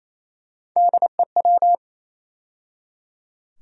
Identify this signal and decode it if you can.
Morse code, heard as a keyed tone, about 28 wpm, 700 Hz. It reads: DEW
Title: USA Call Sign Postfixes - 3 Letters 28wpm